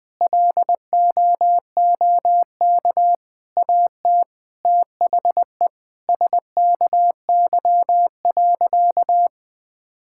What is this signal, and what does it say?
Morse code, heard as a keyed tone, about 21 wpm, 700 Hz. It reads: LOOK AT THE SKY.
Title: Sentences with 9-Word Phrases - Spoken Beginning - 20wpm